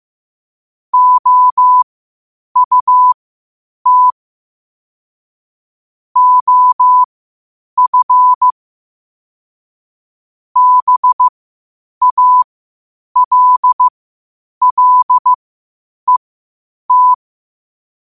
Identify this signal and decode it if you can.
Morse code, heard as a keyed tone, about 15 wpm, spaced as for 8 wpm, 1000 Hz. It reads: OUT OF BALLET
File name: qotdfiles / 8wpm